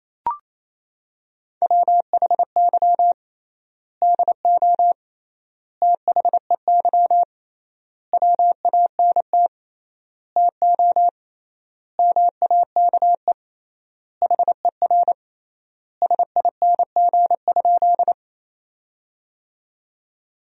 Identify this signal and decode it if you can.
Morse code, heard as a keyed tone, about 28 wpm, 700 Hz. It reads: WHY DO THEY WANT TO MAKE HER SING?
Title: 3x Word-Spacing - Sentences from Top 400 Words - 28wpm